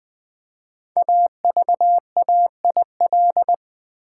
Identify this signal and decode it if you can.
Morse code, heard as a keyed tone, about 20 wpm, 700 Hz. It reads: AVAIL
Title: Build A Word - 5 Letters At A Time - 12-Letter Words - 20wpm